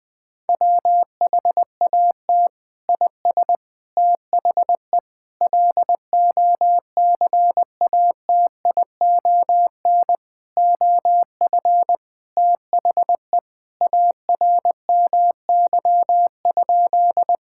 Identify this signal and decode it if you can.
Morse code, heard as a keyed tone, about 20 wpm, 700 Hz. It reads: WHAT IS THE LOCATION OF THE ARMY?
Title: Sentences from Top 1000 Words - Review Part 6 - 20wpm